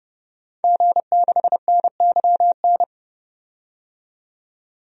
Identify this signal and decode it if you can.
Morse code, heard as a keyed tone, about 30 wpm, 700 Hz. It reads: G6NYN
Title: DXCC-100 Call Signs - 5 Characters - 30wpm